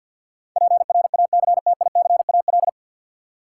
Morse code, heard as a keyed tone, about 53 wpm, 700 Hz. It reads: PRACTICAL
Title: Sentences from Top 1400 Words - New Word Drill Alt - 50wpm